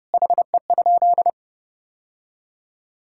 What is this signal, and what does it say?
Morse code, heard as a keyed tone, about 32 wpm, 700 Hz. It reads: HE?
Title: Sentences from Top 1000 Words - Review Part 5 - 30wpm